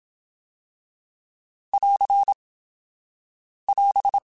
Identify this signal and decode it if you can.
Morse code, heard as a keyed tone, about 27 wpm, 800 Hz. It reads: <AR> <AS>